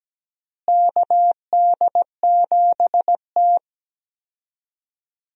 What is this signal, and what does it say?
Morse code, heard as a keyed tone, about 17 wpm, 700 Hz. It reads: KD7T
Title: USA Call Signs - 4 Characters - Repeated 2x - 17wpm